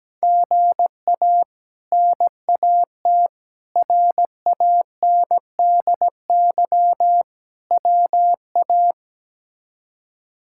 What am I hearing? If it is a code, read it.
Morse code, heard as a keyed tone, about 17 wpm, 700 Hz. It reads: GA NAT RANDY WA
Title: K1USN SST QSO Practice - 17wpm